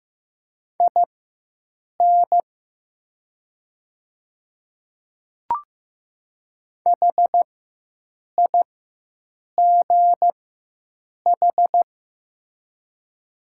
Morse code, heard as a keyed tone, about 15 wpm, spaced as for 7 wpm, 700 Hz. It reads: IN HIGH
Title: 4x Character Spacing - Top 200 Words - 15wpm